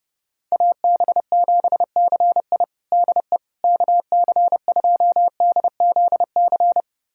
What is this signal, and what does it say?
Morse code, heard as a keyed tone, about 30 wpm, 700 Hz. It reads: AB7CI DE KC2DZC
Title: USA Callsign DE Callsign 30wpm